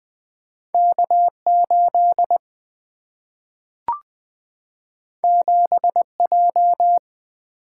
Morse code, heard as a keyed tone, about 20 wpm, 700 Hz. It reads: K8 7J
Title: Single-Letter Single-Number 20wpm